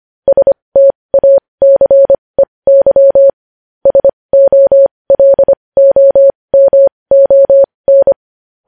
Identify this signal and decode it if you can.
Morse code, heard as a keyed tone, about 25 wpm, spaced as for 20 wpm, 550 Hz. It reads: STACEY SOLOMON
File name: podcast / News headlines-1x-25x20